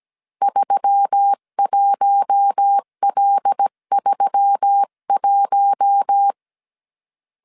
Morse code, heard as a keyed tone, about 17 wpm, 800 Hz. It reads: 31L31
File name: CW / DET / 1 6 1500C S C 16wpm 2733